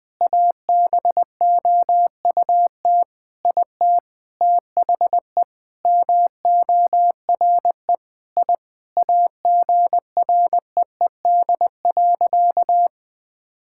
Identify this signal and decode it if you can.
Morse code, heard as a keyed tone, about 20 wpm, 700 Hz. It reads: ABOUT IT THE MORE I AGREED.
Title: Sentences with 8-Word Phrases - Spoken Beginning - 20wpm